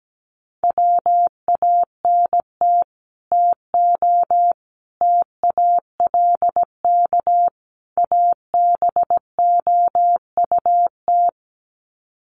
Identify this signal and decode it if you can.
Morse code, heard as a keyed tone, about 17 wpm, 700 Hz. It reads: WANT TO TALK ABOUT
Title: Sets of 4 Words 17wpm